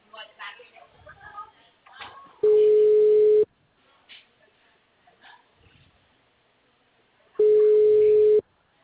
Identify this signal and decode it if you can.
Ringback tone (Europe, 425 Hz), with speech.